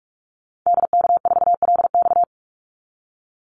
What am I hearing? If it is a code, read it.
Morse code, heard as a keyed tone, about 44 wpm, 700 Hz. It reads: DK4LX
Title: Call Signs - 45wpm